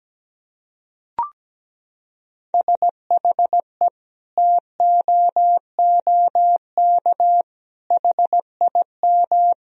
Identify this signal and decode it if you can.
Morse code, heard as a keyed tone, about 17 wpm, 700 Hz. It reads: SHE TOOK HIM
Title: Sentences from Top 1000 Words - Review Part 4 - 17wpm